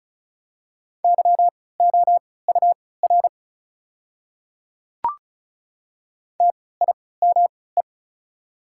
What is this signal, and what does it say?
Morse code, heard as a keyed tone, about 36 wpm, spaced as for 20 wpm, 700 Hz. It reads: YOUR TIME
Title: 3x Character Spacing - Top 100 Words - 35wpm